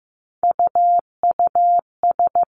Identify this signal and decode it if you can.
Morse code, heard as a keyed tone, about 15 wpm, 700 Hz. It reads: UUS
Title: Three Challenging Letter-Number Combinations 15wpm